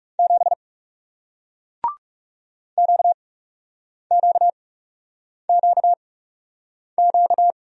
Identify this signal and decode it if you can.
Morse code, heard as a keyed tone, about 40 wpm, spaced as for 8 wpm, 700 Hz. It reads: 7 QQQQ